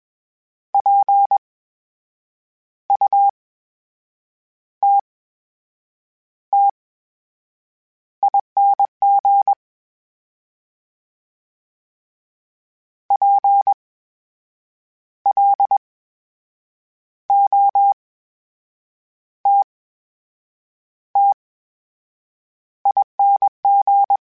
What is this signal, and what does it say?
Morse code, heard as a keyed tone, about 21 wpm, spaced as for 7 wpm, 800 Hz. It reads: PUTTING PLOTTING
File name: Special INGs-mp3s / Special INGs-mp3s / VerySlowToFaster Ing @20